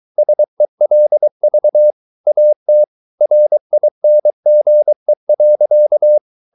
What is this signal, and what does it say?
Morse code, heard as a keyed tone, about 23 wpm, 600 Hz. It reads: SELV AT RINGE.
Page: Ferie 23 wpm